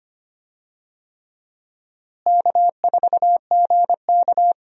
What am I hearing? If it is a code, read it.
Morse code, heard as a keyed tone, about 25 wpm, 700 Hz. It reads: K4GK
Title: USA Call Signs - 4 Characters - Repeated 3x - 25wpm